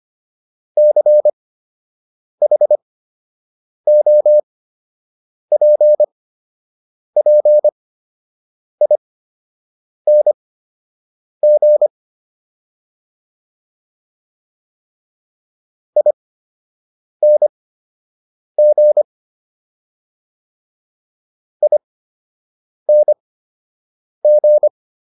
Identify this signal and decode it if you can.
Morse code, heard as a keyed tone, about 25 wpm, spaced as for 7 wpm, 600 Hz. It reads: CHOPPING ING ING